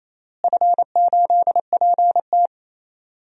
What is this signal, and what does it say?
Morse code, heard as a keyed tone, about 28 wpm, 700 Hz. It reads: F8PT
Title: DXCC-100 Call Signs - 4 Characters - Repeated 3x - 28wpm